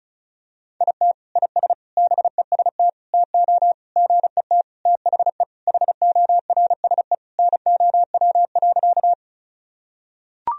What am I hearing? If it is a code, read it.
Morse code, heard as a keyed tone, about 34 wpm, 700 Hz. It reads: IT IS BEST TO GET THE HORSE NOW.
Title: Sentences from Top 1000 Words - Review Part 1 - 35wpm